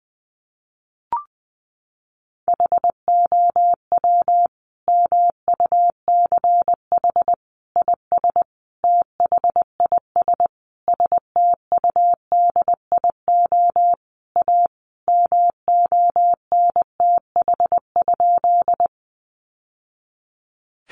Morse code, heard as a keyed tone, about 20 wpm, 700 Hz. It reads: HOW MUCH IS THIS STUDIO A MONTH?
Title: Sentences from Top 1200 Words 20wpm